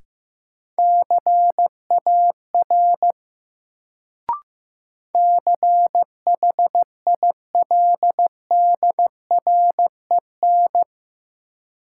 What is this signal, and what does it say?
Morse code, heard as a keyed tone, about 16 wpm, 700 Hz. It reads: CAR CHILDREN
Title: Sentences from Top 300 Words - New Word Drill - 15wpm